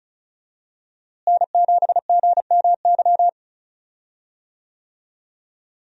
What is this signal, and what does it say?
Morse code, heard as a keyed tone, about 36 wpm, 700 Hz. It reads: N7GMY
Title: USA Call Signs - 5 Characters - Repeated 3x - 35wpm